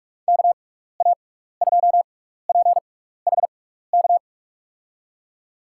Morse code, heard as a keyed tone, about 46 wpm, spaced as for 16 wpm, 700 Hz. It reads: KA2PHK
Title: USA Call Signs - 6 Characters - 6x Character Spacing - 45wpm